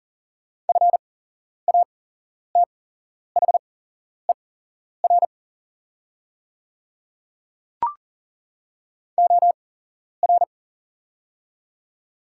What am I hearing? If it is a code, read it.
Morse code, heard as a keyed tone, about 40 wpm, spaced as for 11 wpm, 690 Hz. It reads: FATHER OR
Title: 8x Character Spacing - Top 200 Words - 40wpm